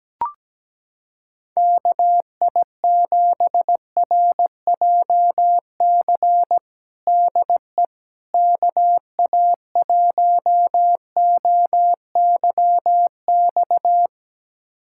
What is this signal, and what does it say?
Morse code, heard as a keyed tone, about 17 wpm, 700 Hz. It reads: KI7RJC DE KA1OYX
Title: USA Callsign DE Callsign - 6 Characters - 17wpm